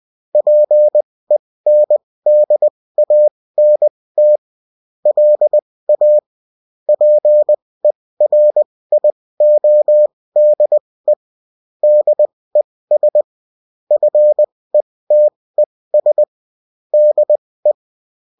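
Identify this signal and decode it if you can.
Morse code, heard as a keyed tone, about 20 wpm, spaced as for 16 wpm, 600 Hz. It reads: PENDANT LA PERIODE DES FETES DE
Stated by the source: Bulletin in text and CW form.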